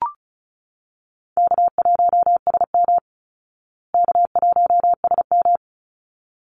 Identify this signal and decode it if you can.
Morse code, heard as a keyed tone, about 34 wpm, 700 Hz. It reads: K1SM K1SM
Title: Call Signs Repeated 3x - 35wpm